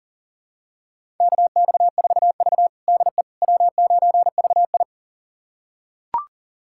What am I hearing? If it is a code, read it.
Morse code, heard as a keyed tone, about 40 wpm, 700 Hz. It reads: KX4V DE W9VI
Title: USA Callsign DE Callsign - 4 Characters - 40wpm